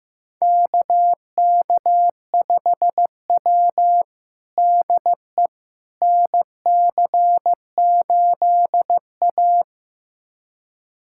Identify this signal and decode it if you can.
Morse code, heard as a keyed tone, about 16 wpm, 700 Hz. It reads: KK5W DE NC8A
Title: USA Callsign DE Callsign - 4 Characters - 15wpm